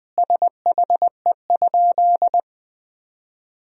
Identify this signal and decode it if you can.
Morse code, heard as a keyed tone, about 21 wpm, 700 Hz. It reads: SHE?
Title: 3x Word-Spacing - Sentences from Top 100 Words - 20wpm